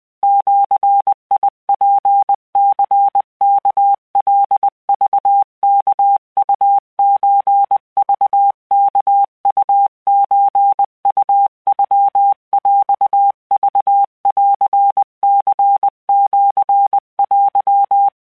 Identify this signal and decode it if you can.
Morse code (Japanese Wabun code), heard as a keyed tone, about 20 wpm, 800 Hz. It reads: ジツニワカクワウソクワウソウノヰクンニシテ